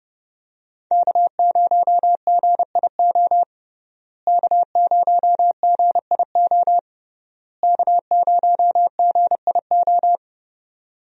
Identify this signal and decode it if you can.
Morse code, heard as a keyed tone, about 30 wpm, 700 Hz. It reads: K0GIO K0GIO K0GIO
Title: USA Call Signs - 5 Characters - Repeated 3x - 30wpm